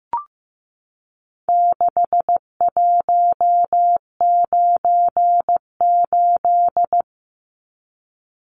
Morse code, heard as a keyed tone, about 15 wpm, 700 Hz. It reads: 6198
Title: Numbers - 4-Digit 15wpm